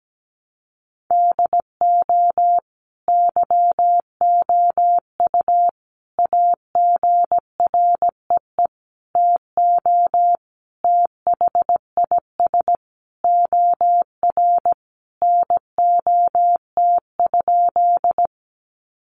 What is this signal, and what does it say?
Morse code, heard as a keyed tone, about 17 wpm, 700 Hz. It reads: DO YOU AGREE TO THIS OR NOT?
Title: Sentences from Top 1100 Words 17wpm